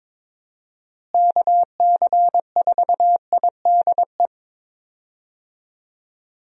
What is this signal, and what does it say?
Morse code, heard as a keyed tone, about 22 wpm, 700 Hz. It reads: KC4IDE
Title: USA Call Signs - 6 Characters - Repeated 2x - 22wpm